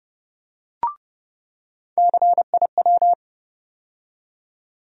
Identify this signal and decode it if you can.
Morse code, heard as a keyed tone, about 32 wpm, 700 Hz. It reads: CIW
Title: USA Call Sign Postfixes - 3 Letters 30wpm